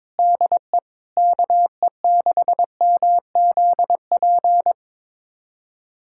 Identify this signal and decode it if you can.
Morse code, heard as a keyed tone, about 22 wpm, 700 Hz. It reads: DE KE6MZP
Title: USA Callsign DE Callsign - 6 Characters - 22wpm